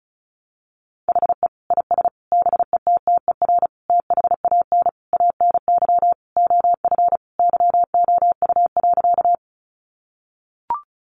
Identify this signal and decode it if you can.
Morse code, heard as a keyed tone, about 34 wpm, 700 Hz. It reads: HE IS BETTER THAN ANY OF YOU.
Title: Sentences from Top 400 Words 35wpm